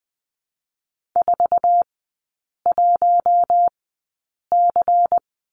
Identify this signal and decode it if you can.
Morse code, heard as a keyed tone, about 20 wpm, spaced as for 8 wpm, 700 Hz. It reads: 41C
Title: 152 Lesson 25e - Letter p - Three Characters 20wpm